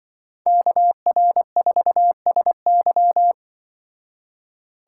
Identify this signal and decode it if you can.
Morse code, heard as a keyed tone, about 24 wpm, 700 Hz. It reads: KR4SY
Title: Speed Racing - USA Call Signs 20wpm